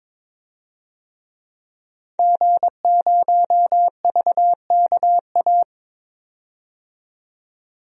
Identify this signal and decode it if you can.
Morse code, heard as a keyed tone, about 21 wpm, 700 Hz. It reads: G0VKA
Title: DXCC-100 Call Signs - 5 Characters - 22wpm